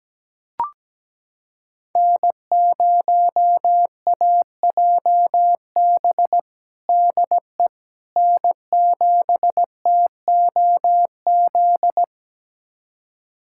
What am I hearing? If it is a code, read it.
Morse code, heard as a keyed tone, about 17 wpm, 700 Hz. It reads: N0AJB DE N7TOZ
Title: USA Callsign DE Callsign - 5 Characters - 17wpm